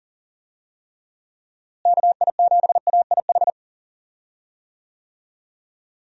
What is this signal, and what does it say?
Morse code, heard as a keyed tone, about 40 wpm, 700 Hz. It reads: KI7AIH
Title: USA Call Signs - 6 Characters - Repeated 3x - 40wpm